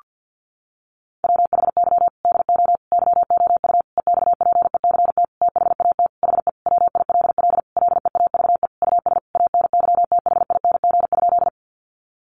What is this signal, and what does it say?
Morse code, heard as a keyed tone, about 48 wpm, 700 Hz. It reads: WHY DO YOU EXPECT THAT HE WILL LEAVE US ANYTHING?
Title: Sentences from Top 1100 Words 50wpm